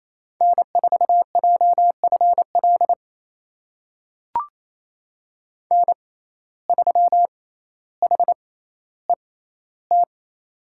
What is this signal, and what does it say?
Morse code, heard as a keyed tone, about 28 wpm, spaced as for 14 wpm, 700 Hz. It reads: N4JFL N3HET
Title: USA Call Signs - 5 Characters - 6x Character Spacing - 28wpm